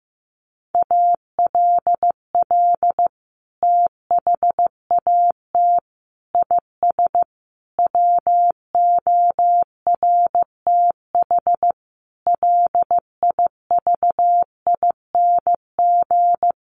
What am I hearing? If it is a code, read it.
Morse code, heard as a keyed tone, about 15 wpm, 700 Hz. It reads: ALL THAT IS WORTH LIVING
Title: Sentences from Top 1100 Words 15wpm